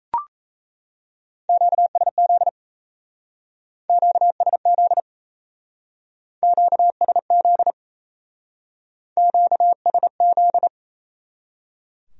Morse code, heard as a keyed tone, about 33 wpm, 700 Hz. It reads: QSZ QSZ QSZ QSZ
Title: Speed-Racing Q-Codes - 28wpm